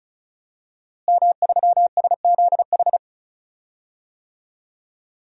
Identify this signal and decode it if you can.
Morse code, heard as a keyed tone, about 36 wpm, 700 Hz. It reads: M3SZH
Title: DXCC-100 Call Signs - 5 Characters - Repeated 3x - 35wpm